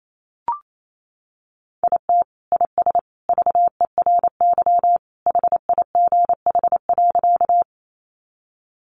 Morse code, heard as a keyed tone, about 28 wpm, 700 Hz. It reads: IT IS VERY HIGH.
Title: Sentences from Top 200 Words 28wpm